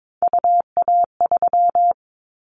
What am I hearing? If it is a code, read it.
Morse code, heard as a keyed tone, about 22 wpm, 700 Hz. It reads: UA3
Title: Callsign Prefixes for Top 50 DXCC - 22wpm